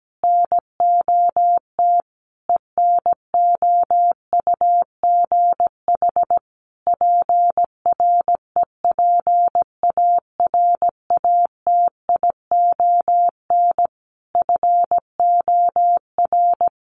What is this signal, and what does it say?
Morse code, heard as a keyed tone, about 17 wpm, 700 Hz. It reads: NOT ENOUGH PREPARATION FOR
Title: Sentences from Top 1400 Words 17wpm